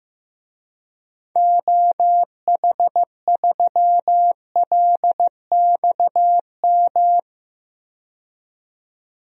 Morse code, heard as a keyed tone, about 15 wpm, 700 Hz. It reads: OH3LXM
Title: DXCC-100 Call Signs - 6 Characters - 15wpm